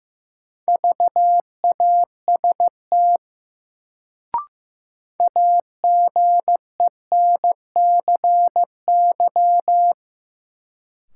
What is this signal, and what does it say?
Morse code, heard as a keyed tone, about 15 wpm, 700 Hz. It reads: VAST AGENCY
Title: Sentences from Top 1300 Words - New Word Drill Alt - 15wpm